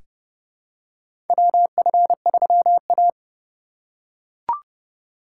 Morse code, heard as a keyed tone, about 31 wpm, 700 Hz. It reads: WF3A
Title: USA Call Signs - 4 Characters - 8x Character Spacing - 30wpm